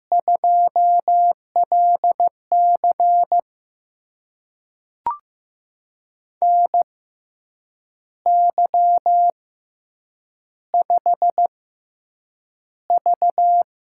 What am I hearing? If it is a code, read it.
Morse code, heard as a keyed tone, about 15 wpm, spaced as for 7 wpm, 700 Hz. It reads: 2LC NY5V
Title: USA Call Signs - 4 Characters - 6x Character Spacing - 15wpm